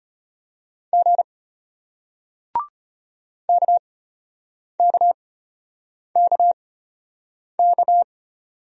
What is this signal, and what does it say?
Morse code, heard as a keyed tone, about 33 wpm, spaced as for 8 wpm, 700 Hz. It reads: G KKKK